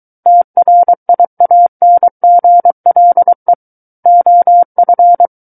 Morse code, heard as a keyed tone, about 23 wpm, 700 Hz. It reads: TRIANGLE OF
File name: Book One / 14-Chapter 14 - 23 WPM